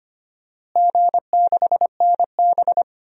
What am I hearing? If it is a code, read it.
Morse code, heard as a keyed tone, about 25 wpm, 700 Hz. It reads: G6NB
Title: DXCC-100 Call Signs - 4 Characters - Repeated 2x - 25wpm